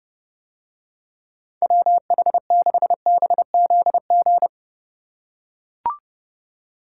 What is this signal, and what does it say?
Morse code, heard as a keyed tone, about 30 wpm, 700 Hz. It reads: WH6BZG